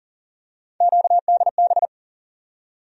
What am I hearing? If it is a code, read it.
Morse code, heard as a keyed tone, about 40 wpm, 700 Hz. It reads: QDB
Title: USA Call Sign Postfixes - 3 Letters 40wpm